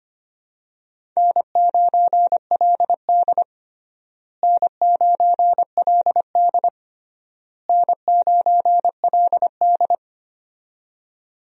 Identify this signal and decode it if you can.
Morse code, heard as a keyed tone, about 25 wpm, 700 Hz. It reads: N9LD N9LD N9LD
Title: USA Call Signs - 4 Characters - Repeated 3x - 25wpm